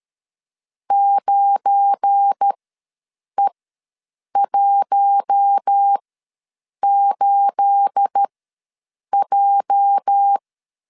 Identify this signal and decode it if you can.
Morse code, heard as a keyed tone, about 13 wpm, spaced as for 7 wpm, 800 Hz. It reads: 9E18J
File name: LEZIONI / 9 286 ws 12wpm 3351